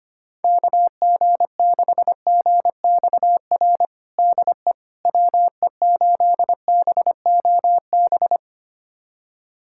Morse code, heard as a keyed tone, about 25 wpm, 700 Hz. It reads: KG6GXR DE WE8BOB
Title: USA Callsign DE Callsign 25wpm